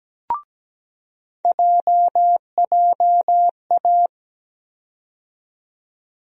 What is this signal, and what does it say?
Morse code, heard as a keyed tone, about 17 wpm, 700 Hz. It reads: JJA
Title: USA Call Sign Postfixes - 3 Letters 17wpm